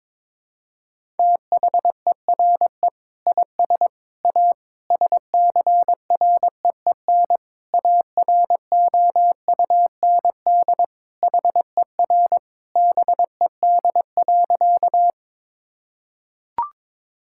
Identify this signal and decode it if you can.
Morse code, heard as a keyed tone, about 22 wpm, 700 Hz. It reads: THERE IS A SCREEN AROUND HER BED.
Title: Sentences from Top 1000 Words - Review Part 6 - 22wpm